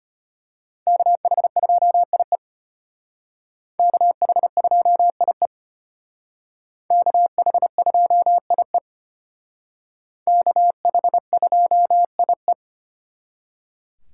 Morse code, heard as a keyed tone, about 32 wpm, 700 Hz. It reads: KH2IE KH2IE KH2IE KH2IE